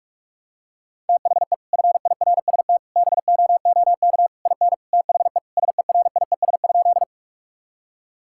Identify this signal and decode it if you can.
Morse code, heard as a keyed tone, about 46 wpm, 700 Hz. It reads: THE FIRST BOOK IN THE SERIES?
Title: Sentences from Top 1000 Words - Review Part 5 - 45wpm